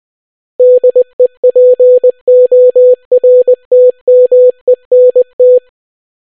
Morse code, heard as a keyed tone, about 20 wpm, 500 Hz. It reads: DEPORTMENT